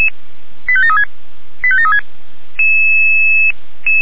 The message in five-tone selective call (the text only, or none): 85736 85736